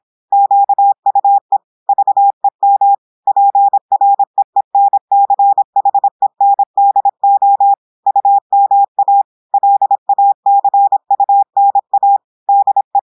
Morse code, heard as a keyed tone, about 26 wpm, 800 Hz. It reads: QUE VEM PREENCHENDO UMA LACUNA DE
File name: audios / portugais / portugais texte en clair 1 vitesse 26 mots minutes